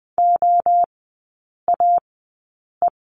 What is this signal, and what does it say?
Morse code, heard as a keyed tone, about 20 wpm, spaced as for 8 wpm, 700 Hz. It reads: OAE
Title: 047 Lesson 9e - Number 4 - Three Characters 20wpm